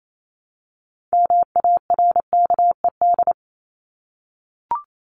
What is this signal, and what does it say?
Morse code, heard as a keyed tone, about 28 wpm, 700 Hz. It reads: MARKED
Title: ED Suffix - Top 100 Words with Base Words 28wpm